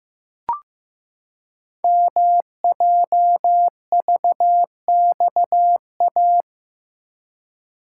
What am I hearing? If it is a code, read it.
Morse code, heard as a keyed tone, about 15 wpm, 700 Hz. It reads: MJVXA
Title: Code Groups - 5 Letters - 15wpm